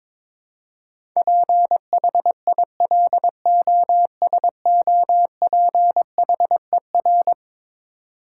Morse code, heard as a keyed tone, about 22 wpm, 700 Hz. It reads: PHILOSOPHER
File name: practice / 22wpm / Eleven Letter Words - 22wpm